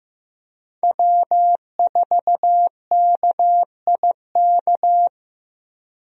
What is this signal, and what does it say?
Morse code, heard as a keyed tone, about 15 wpm, 700 Hz. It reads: W4KIK
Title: USA Call Signs - 5 Characters - Repeated 2x - 15wpm